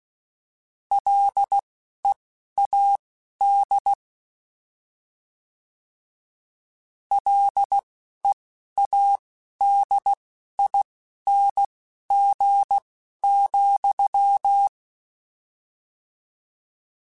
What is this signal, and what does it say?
Morse code, heard as a keyed tone, about 16 wpm, spaced as for 11 wpm, 800 Hz. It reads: LEAD LEADING,